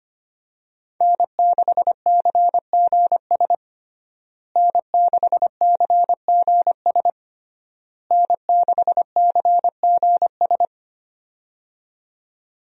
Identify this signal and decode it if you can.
Morse code, heard as a keyed tone, about 25 wpm, 700 Hz. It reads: N6CGS N6CGS N6CGS
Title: USA Call Signs - 5 Characters - Repeated 3x - 25wpm